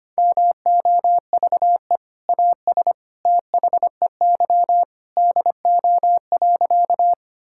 Morse code, heard as a keyed tone, about 25 wpm, 700 Hz. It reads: MOVE AS THEY DO.
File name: practice / 25wpm / Sentences with 2-Word Phrases - 25wpm